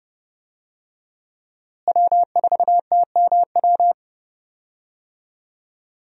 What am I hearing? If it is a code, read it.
Morse code, heard as a keyed tone, about 30 wpm, 700 Hz. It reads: W4TMW
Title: USA Call Signs - 5 Characters - Repeated 2x - 30wpm